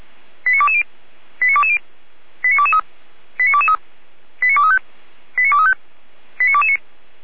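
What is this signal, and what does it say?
Five-tone selective call (ZVEI), heard as quick runs of five tones, tones of about 70 ms. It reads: 89220 89220 89223 89223 89235 89235 89229
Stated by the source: ZVEI five-tone call received on an SDR.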